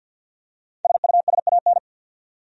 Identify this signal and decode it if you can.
Morse code, heard as a keyed tone, about 52 wpm, 700 Hz. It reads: SUSAN
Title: USA Common Names - Max 5 Letters - 50wpm